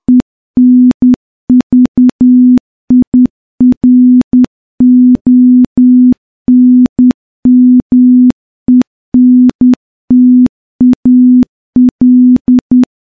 Morse code, heard as a keyed tone, about 10 wpm, 250 Hz. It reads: ENVIRONMENTAL
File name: morse-training-10WPM